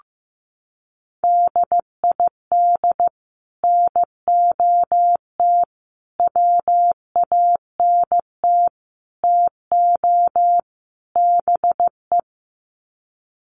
Morse code, heard as a keyed tone, about 15 wpm, 700 Hz. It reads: DID NOT WANT TO BE
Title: Sets of 5 Words Encore 15wpm